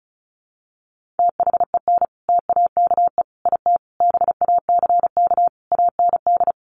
Morse code, heard as a keyed tone, about 34 wpm, 700 Hz. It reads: THEN TAKE IT BACK AND
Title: Sentences from Top 1500 Words 35wpm